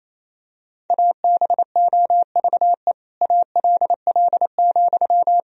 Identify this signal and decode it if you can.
Morse code, heard as a keyed tone, about 27 wpm, 700 Hz. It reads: ABOVE ALL,
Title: Sentences from Top 1000 Words - Review Part 4 - 28wpm